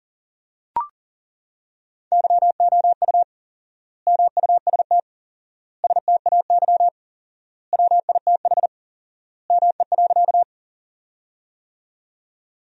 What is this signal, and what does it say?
Morse code, heard as a keyed tone, about 40 wpm, 700 Hz. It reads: YOU MUST STAY WITH ME.
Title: 4x Word-Spacing - Sentences from Top 500 Words - 40wpm